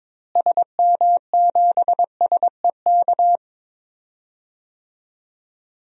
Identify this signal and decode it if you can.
Morse code, heard as a keyed tone, about 22 wpm, 700 Hz. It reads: SM7SEK
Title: DXCC-100 Call Signs - 6 Characters - Repeated 3x - 22wpm